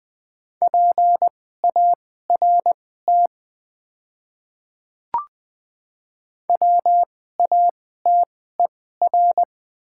Morse code, heard as a keyed tone, about 20 wpm, spaced as for 14 wpm, 700 Hz. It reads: PART WATER
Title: 2x Character Spacing - Top 300 Words - 20wpm